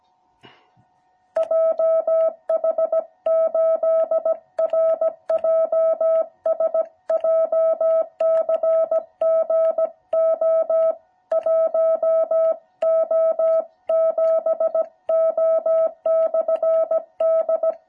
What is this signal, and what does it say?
Morse code, heard as a keyed tone, about 17 wpm, 650 Hz. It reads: JH8RJS JCG O1O7O/D